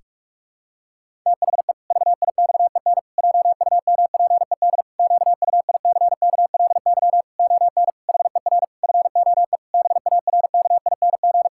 Morse code, heard as a keyed tone, about 47 wpm, 700 Hz. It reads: THE VIXEN JUMPED QUICKLY ON HER FOE BARKING
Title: Pangram Challenge v2 - 45wpm